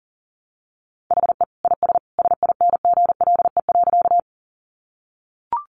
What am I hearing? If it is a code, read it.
Morse code, heard as a keyed tone, about 40 wpm, 700 Hz. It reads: HE IS SINGLE.
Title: Sentences from Top 700 Words 40wpm